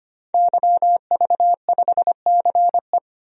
Morse code, heard as a keyed tone, about 25 wpm, 700 Hz. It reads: YV5CE
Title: DXCC-100 Call Signs - 5 Characters - 25wpm